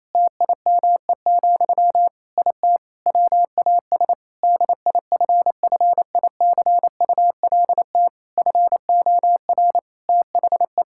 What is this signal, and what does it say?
Morse code, heard as a keyed tone, about 27 wpm, 700 Hz. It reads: TIME, IT WAS DIFFICULT FOR THE
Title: Sentences from Top 1000 Words - Review Part 5 - 28wpm